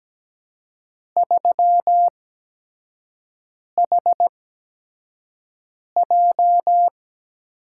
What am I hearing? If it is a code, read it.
Morse code, heard as a keyed tone, about 17 wpm, spaced as for 5 wpm, 700 Hz. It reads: 3HJ
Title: USA Call Signs - 4 Characters - 8x Character Spacing - 17wpm